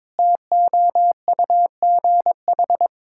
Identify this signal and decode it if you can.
Morse code, heard as a keyed tone, about 22 wpm, 700 Hz. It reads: TOUGH
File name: Difficult Sentences with Hints - 22wpm